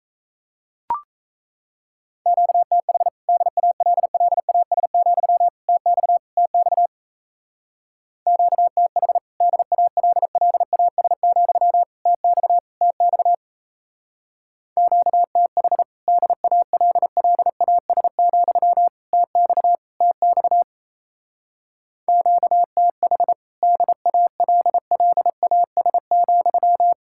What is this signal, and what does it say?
Morse code, heard as a keyed tone, about 34 wpm, 700 Hz. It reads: QTH DALLAS, TX TX QTH DALLAS, TX TX QTH DALLAS, TX TX QTH DALLAS,
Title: Speed-Racing Common QSO Phrases Encore - 28wpm